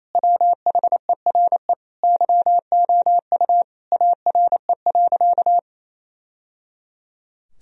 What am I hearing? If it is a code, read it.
Morse code, heard as a keyed tone, about 28 wpm, 700 Hz. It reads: WHERE YOU ARE.
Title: Sentences from Top 1000 Words - Review Part 1 - 28wpm